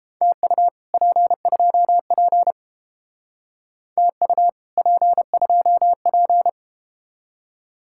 Morse code, heard as a keyed tone, about 31 wpm, 700 Hz. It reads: TU P2P TU P2P